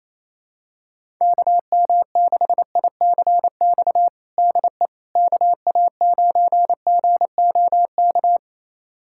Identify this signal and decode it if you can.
Morse code, heard as a keyed tone, about 28 wpm, 700 Hz. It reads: KM6ICX DE KA9GOK
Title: USA Callsign DE Callsign - 6 Characters - 28wpm